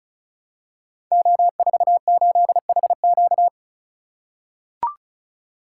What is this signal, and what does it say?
Morse code, heard as a keyed tone, about 36 wpm, 700 Hz. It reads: O48HQ
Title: Code Groups - 5 Letter:Numbers - 35wpm